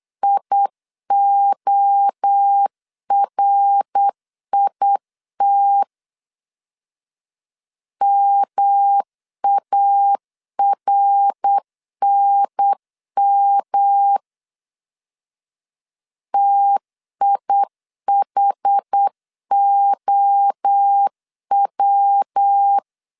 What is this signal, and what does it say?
Morse code, heard as a keyed tone, about 8 wpm, 800 Hz. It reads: IORIT MARNM TIHOW